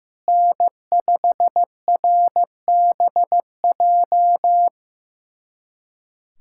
Morse code, heard as a keyed tone, about 15 wpm, 700 Hz. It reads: N5RBJ
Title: USA Callsign DE Callsign 15wpm